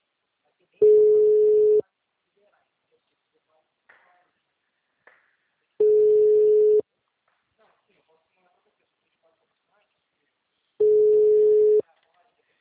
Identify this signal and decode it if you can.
Ringback tone (Europe, 425 Hz), with speech.